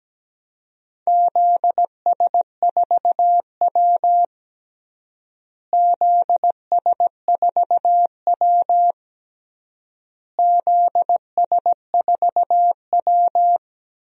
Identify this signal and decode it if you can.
Morse code, heard as a keyed tone, about 17 wpm, 700 Hz. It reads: ZS4W ZS4W ZS4W
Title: DXCC-100 Call Signs - 4 Characters - Repeated 3x - 17wpm